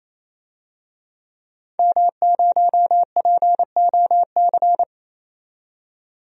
Morse code, heard as a keyed tone, about 28 wpm, 700 Hz. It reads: M0POC
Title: DXCC-100 Call Signs - 5 Characters - 28wpm